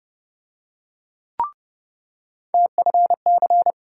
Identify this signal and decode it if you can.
Morse code, heard as a keyed tone, about 30 wpm, 700 Hz. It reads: TFC
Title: USA Call Sign Postfixes - 3 Letters 30wpm